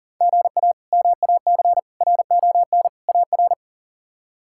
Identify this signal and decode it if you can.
Morse code, heard as a keyed tone, about 38 wpm, 700 Hz. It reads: GA MAC RON AR
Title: K1USN SST QSO Practice - 40wpm